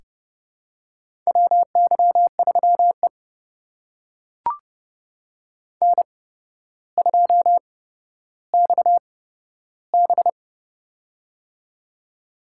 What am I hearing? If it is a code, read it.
Morse code, heard as a keyed tone, about 29 wpm, spaced as for 13 wpm, 700 Hz. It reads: WY3E N2XB
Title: USA Call Signs - 4 Characters - 8x Character Spacing - 30wpm